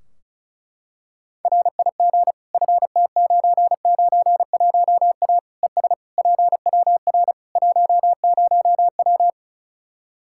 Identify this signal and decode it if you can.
Morse code, heard as a keyed tone, about 34 wpm, 700 Hz. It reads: RIG FT991A ES PWR 10W
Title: Practice QSO - Exchange 2 - Info - 35wpm